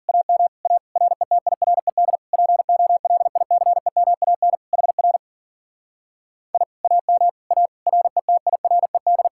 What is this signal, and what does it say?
Morse code, heard as a keyed tone, about 52 wpm, 700 Hz. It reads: AM A RETIRED POLICEMAN HR I AM A RETIRED